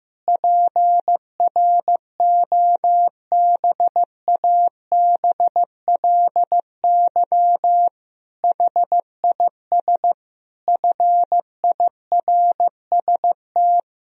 Morse code, heard as a keyed tone, about 16 wpm, 700 Hz. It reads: PROBABLY HIS FIRST
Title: Sentences from Top 1000 Words - Review Part 4 - 15wpm